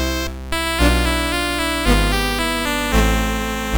There is no code.